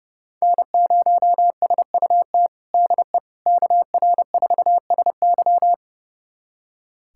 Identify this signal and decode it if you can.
Morse code, heard as a keyed tone, about 30 wpm, 700 Hz. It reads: N0SUT DE KR4SY
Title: USA Callsign DE Callsign - 5 Characters - 30wpm